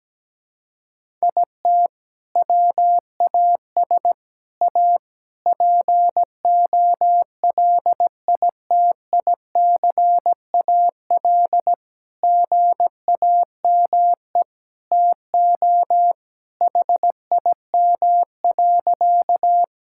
Morse code, heard as a keyed tone, about 17 wpm, 700 Hz. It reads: IT WAS A POLITICAL GAME TO HIM.
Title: Sentences from Top 1000 Words - Review Part 5 - 17wpm